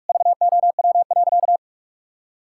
Morse code, heard as a keyed tone, about 46 wpm, 700 Hz. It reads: VOW.
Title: Pangram Challenge v2 - 45wpm